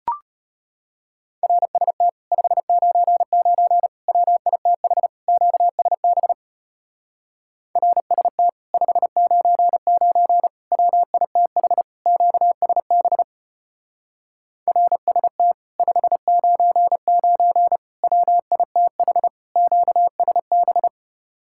Morse code, heard as a keyed tone, about 33 wpm, 700 Hz. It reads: RST 599 WITH QSB RST 599 WITH QSB RST 599 WITH QSB